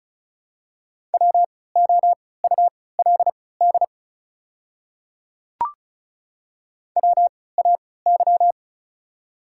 Morse code, heard as a keyed tone, about 36 wpm, spaced as for 20 wpm, 690 Hz. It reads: WOULD WAY
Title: 3x Character Spacing - Top 100 Words - 35wpm